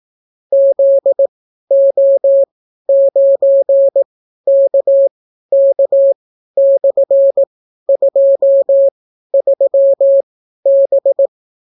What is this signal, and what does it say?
Morse code, heard as a keyed tone, about 18 wpm, spaced as for 12 wpm, 550 Hz. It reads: ZO9KK/23B